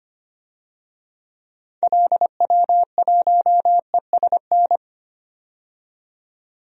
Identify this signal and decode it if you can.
Morse code, heard as a keyed tone, about 25 wpm, 700 Hz. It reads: LW1ESN
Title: DXCC-100 Call Signs - 6 Characters - 25wpm